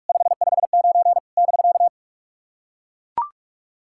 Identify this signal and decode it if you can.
Morse code, heard as a keyed tone, about 44 wpm, 700 Hz. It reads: 559 <BK>
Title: SOTA QSO Practice 45wpm